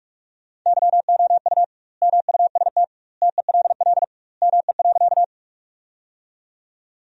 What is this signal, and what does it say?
Morse code, heard as a keyed tone, about 46 wpm, 700 Hz. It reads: YOU MUST TELL ME.
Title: 2x Word-Spacing - Sentences from Top 200 Words - 45wpm